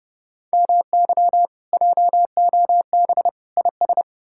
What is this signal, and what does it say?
Morse code, heard as a keyed tone, about 40 wpm, 700 Hz. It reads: MY JOB IS